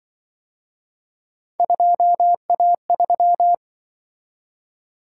Morse code, heard as a keyed tone, about 25 wpm, 700 Hz. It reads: 2A3
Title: Speed-Racing - Three Letter-Number Combinations - 20wpm